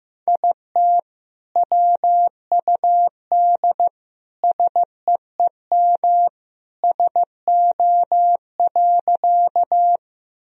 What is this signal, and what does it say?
Morse code, heard as a keyed tone, about 15 wpm, 700 Hz. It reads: IT WUD SEEM SO.
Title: Sentences from Top 300 Words and Abbreviations 15wpm